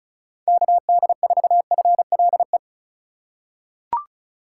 Morse code, heard as a keyed tone, about 36 wpm, 700 Hz. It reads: KD4FLE
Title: USA Call Signs - 6 Characters - 4x Character Spacing - 35wpm